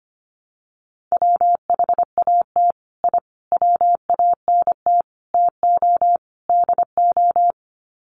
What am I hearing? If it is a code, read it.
Morse code, heard as a keyed tone, about 25 wpm, 700 Hz. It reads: WHAT I WANT TO DO
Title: Sets of 5 Words Encore 25wpm